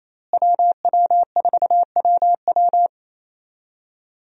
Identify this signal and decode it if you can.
Morse code, heard as a keyed tone, about 28 wpm, 700 Hz. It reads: WW4WW
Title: USA Call Signs - 5 Characters - Repeated 2x - 28wpm